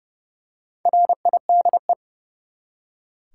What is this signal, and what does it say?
Morse code, heard as a keyed tone, about 30 wpm, 700 Hz. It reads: RIDE
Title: Four Letter Words - 30wpm